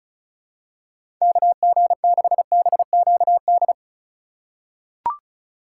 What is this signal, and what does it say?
Morse code, heard as a keyed tone, about 36 wpm, 700 Hz. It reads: KG6BQD
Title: USA Call Signs - 6 Characters - 3x Character Spacing - 35wpm